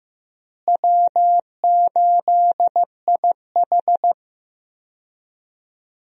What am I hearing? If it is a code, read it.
Morse code, heard as a keyed tone, about 15 wpm, 700 Hz. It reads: W8IH
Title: USA Call Signs - 4 Characters - Repeated 2x - 15wpm